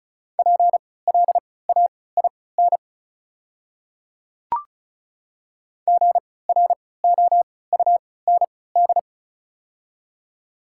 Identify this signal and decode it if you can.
Morse code, heard as a keyed tone, about 36 wpm, spaced as for 20 wpm, 700 Hz. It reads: PLAIN GROUND
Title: 3x Character Spacing - Top 500 Words - 35wpm